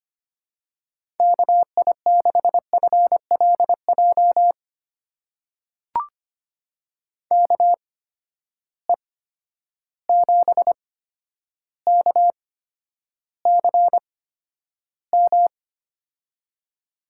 Morse code, heard as a keyed tone, about 25 wpm, spaced as for 11 wpm, 700 Hz. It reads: KI6FLJ KE7KCM
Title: USA Call Signs - 6 Characters - 8x Character Spacing - 25wpm